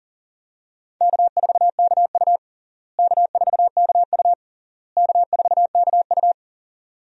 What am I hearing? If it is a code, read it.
Morse code, heard as a keyed tone, about 37 wpm, 700 Hz. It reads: K4KU K4KU K4KU
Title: USA Call Signs - 4 Characters - Repeated 3x - 40wpm